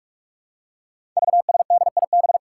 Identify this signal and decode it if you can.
Morse code, heard as a keyed tone, about 44 wpm, 700 Hz. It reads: VSDIB
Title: Five Challenging Letter-Number Combinations 45wpm